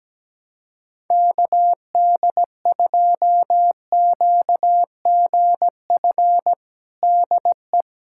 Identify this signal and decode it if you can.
Morse code, heard as a keyed tone, about 17 wpm, 700 Hz. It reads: KD2QGF DE
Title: USA Callsign DE Callsign - 6 Characters - 17wpm